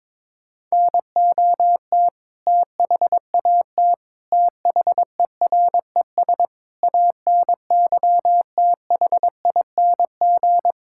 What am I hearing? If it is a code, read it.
Morse code, heard as a keyed tone, about 22 wpm, 700 Hz. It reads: NOT THAT THERES ANYTHING
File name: practice / 22wpm / Sets of 7 Words - 22wpm